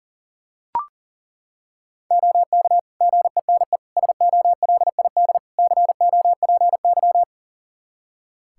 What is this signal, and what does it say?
Morse code, heard as a keyed tone, about 40 wpm, 700 Hz. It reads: OK GENE SOLID COPY
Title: Practice QSO - Exchange 3 - More Info - 40wpm